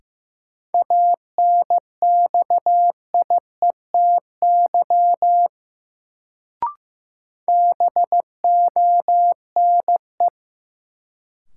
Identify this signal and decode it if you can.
Morse code, heard as a keyed tone, about 15 wpm, 700 Hz. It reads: ANXIETY BONE
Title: Sentences from Top 1600 Words - New Word Drill - 15wpm